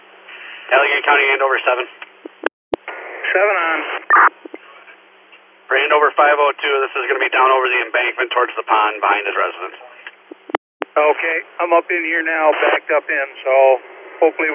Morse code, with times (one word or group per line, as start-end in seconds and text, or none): none